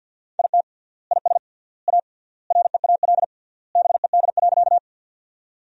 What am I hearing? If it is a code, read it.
Morse code, heard as a keyed tone, about 47 wpm, 700 Hz. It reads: IT IS A REAL BED.
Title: 3x Word-Spacing - Sentences from Top 500 Words - 50wpm